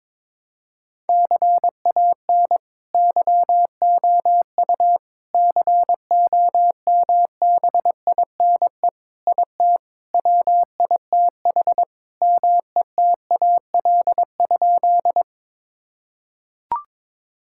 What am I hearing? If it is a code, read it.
Morse code, heard as a keyed tone, about 22 wpm, 700 Hz. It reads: CAN YOU COMBINE IT WITH METAL?
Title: Sentences from Top 1600 Words 22wpm